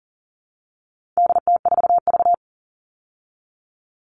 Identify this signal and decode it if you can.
Morse code, heard as a keyed tone, about 39 wpm, 700 Hz. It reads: DT4V
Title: All Cut Numbers - 4-Digit 40wpm